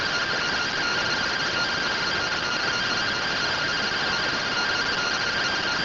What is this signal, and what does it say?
Two-tone teleprinter signal, about 170 Hz shift.